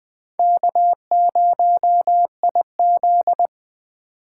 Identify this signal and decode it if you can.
Morse code, heard as a keyed tone, about 23 wpm, 700 Hz. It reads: K0IZ
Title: Speed Racing - USA 1x2 Call Signs 20wpm